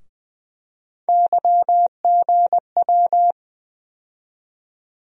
Morse code, heard as a keyed tone, about 20 wpm, 700 Hz. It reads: YGW
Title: USA Call Sign Postfixes - 3 Letters 20wpm